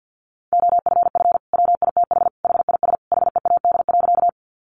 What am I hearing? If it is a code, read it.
Morse code, heard as a keyed tone, about 50 wpm, 700 Hz. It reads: OFF WITH HIS HEAD.
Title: Sentences from Top 300 Words 50wpm